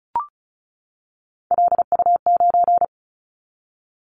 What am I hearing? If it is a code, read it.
Morse code, heard as a keyed tone, about 36 wpm, 700 Hz. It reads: LU9
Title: Callsign Prefixes for Top 50 DXCC - 35wpm